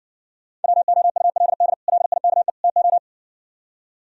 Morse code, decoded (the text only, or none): WOULD LIKE TO